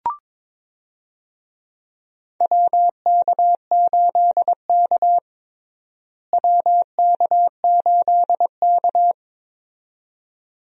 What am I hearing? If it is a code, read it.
Morse code, heard as a keyed tone, about 22 wpm, 700 Hz. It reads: WK8K WK8K
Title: USA Call Signs - 4 Characters - Repeated 2x - 22wpm